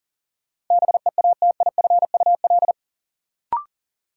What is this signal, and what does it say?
Morse code, heard as a keyed tone, about 40 wpm, 700 Hz. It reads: BEATIFUL
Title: Sentences from Top 1300 Words - New Word Drill - 40wpm